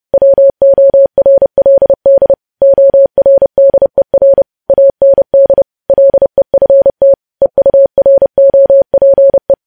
Morse code, heard as a keyed tone, about 30 wpm, 550 Hz. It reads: WORLD ORDER AND LEFT EUROPE
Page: news headlines in morse code at 30 wpm